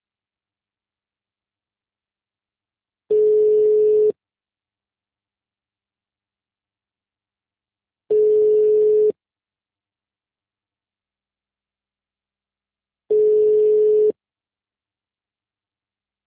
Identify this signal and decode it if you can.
Ringback tone (Europe, 425 Hz).
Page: Leme 2